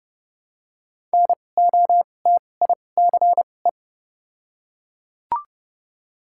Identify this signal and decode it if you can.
Morse code, heard as a keyed tone, about 30 wpm, spaced as for 22 wpm, 700 Hz. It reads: NOTICE
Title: 2x Character Spacing - Top 500 Words - 30wpm